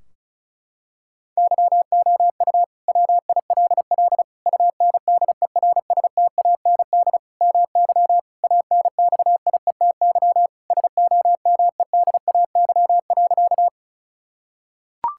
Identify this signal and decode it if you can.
Morse code, heard as a keyed tone, about 34 wpm, 700 Hz. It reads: YOU WILL UNDERSTAND MY ANXIETY SOMEDAY.
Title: Sentences from Top 1600 Words 35wpm